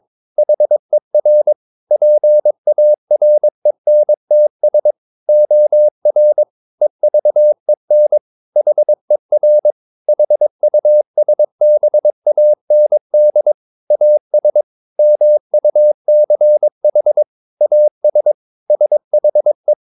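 Morse code, heard as a keyed tone, about 22 wpm, 600 Hz. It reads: HER PARENTS OR EVEN HER HUSBAND AS MUCH AS SHE
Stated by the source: Morse code practice at 22 wpm, sending text from an English novel.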